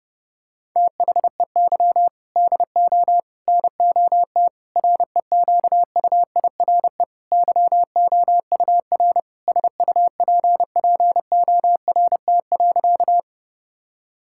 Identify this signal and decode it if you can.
Morse code, heard as a keyed tone, about 32 wpm, 700 Hz. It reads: THEY DO NOT REQUIRE YOUR SUPPORT.
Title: Sentences from Top 1000 Words - Review Part 6 - 30wpm